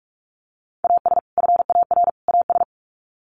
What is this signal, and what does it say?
Morse code, heard as a keyed tone, about 44 wpm, 700 Hz. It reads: AS FAR AS
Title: Sets of 3 Words 45wpm